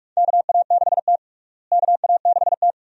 Morse code, heard as a keyed tone, about 46 wpm, 700 Hz. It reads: KA6T KA6T
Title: USA Call Signs - 4 Characters - Repeated 2x - 45wpm